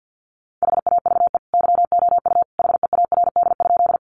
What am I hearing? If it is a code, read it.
Morse code, heard as a keyed tone, about 48 wpm, 700 Hz. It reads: HAVE YOU HEARD?
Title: Sentences from Top 400 Words 50wpm